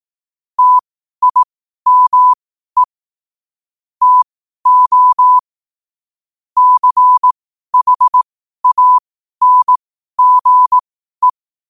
Morse code, heard as a keyed tone, about 18 wpm, spaced as for 12 wpm, 1000 Hz. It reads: TIME TO CHANGE
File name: qotdfiles / 12wpm